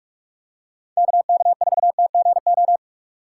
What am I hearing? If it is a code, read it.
Morse code, heard as a keyed tone, about 41 wpm, 700 Hz. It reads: KK4TGO
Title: USA Call Signs - 6 Characters - 6x Character Spacing - 45wpm